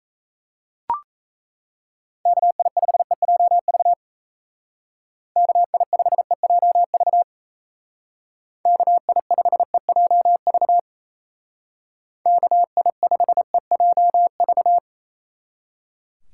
Morse code, heard as a keyed tone, about 35 wpm, 700 Hz. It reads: KI5EJV KI5EJV KI5EJV KI5EJV
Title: Speed Racing - USA Call Signs 28wpm